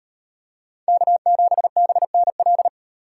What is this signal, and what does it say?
Morse code, heard as a keyed tone, about 38 wpm, 700 Hz. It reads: K7BNL